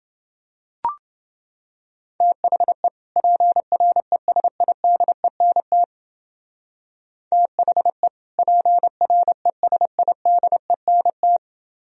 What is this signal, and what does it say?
Morse code, heard as a keyed tone, about 28 wpm, 700 Hz. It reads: THE PRESIDENT THE PRESIDENT